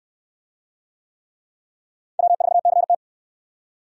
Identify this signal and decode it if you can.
Morse code, heard as a keyed tone, about 64 wpm, 700 Hz. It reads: W4CT